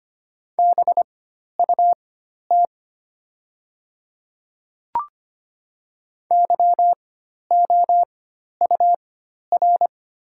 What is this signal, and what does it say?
Morse code, heard as a keyed tone, about 25 wpm, spaced as for 12 wpm, 700 Hz. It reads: BUT YOUR
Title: 4x Character Spacing - Top 500 Words - 25wpm